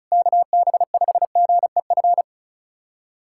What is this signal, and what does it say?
Morse code, heard as a keyed tone, about 36 wpm, 700 Hz. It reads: KB5GEF
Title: USA Callsign DE Callsign - 6 Characters - 35wpm